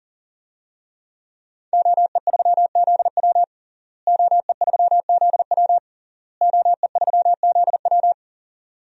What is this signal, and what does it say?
Morse code, heard as a keyed tone, about 40 wpm, 700 Hz. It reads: OE3ZW OE3ZW OE3ZW
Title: DXCC-100 Call Signs - 5 Characters - Repeated 3x - 40wpm